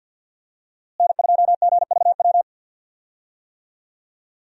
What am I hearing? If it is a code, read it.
Morse code, heard as a keyed tone, about 46 wpm, 700 Hz. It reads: N2GVW
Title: USA Call Signs - 5 Characters - Repeated 3x - 50wpm